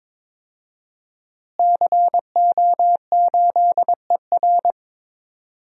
Morse code, heard as a keyed tone, about 22 wpm, 700 Hz. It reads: CO8ER
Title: DXCC-100 Call Signs - 5 Characters - Repeated 2x - 22wpm